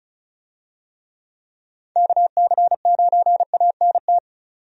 Morse code, heard as a keyed tone, about 36 wpm, 700 Hz. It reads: KC9ANT
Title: USA Call Signs - 6 Characters - 35wpm